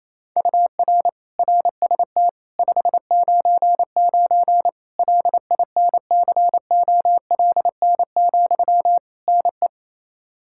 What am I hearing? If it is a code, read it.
Morse code, heard as a keyed tone, about 28 wpm, 700 Hz. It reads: UR RST 599 LINCOLN, NE
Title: Practice QSO - Exchange 1 - Beginning - 28wpm